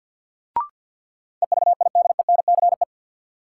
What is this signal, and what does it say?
Morse code, heard as a keyed tone, about 48 wpm, 700 Hz. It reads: EVIDENCE
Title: Sentences from Top 1300 Words - New Word Drill - 50wpm